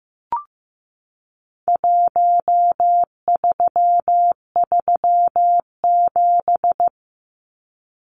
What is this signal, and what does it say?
Morse code, heard as a keyed tone, about 15 wpm, 700 Hz. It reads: 1337
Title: Numbers - 4-Digit 15wpm